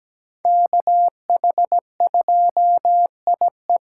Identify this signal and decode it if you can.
Morse code, heard as a keyed tone, about 20 wpm, 700 Hz. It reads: KH2IE
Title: Speed Racing - USA 2x2 Call Signs 17wpm